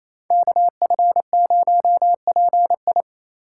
Morse code, heard as a keyed tone, about 28 wpm, 700 Hz. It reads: KF0PI
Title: USA Call Signs - 8x Character Spacing - 28wpm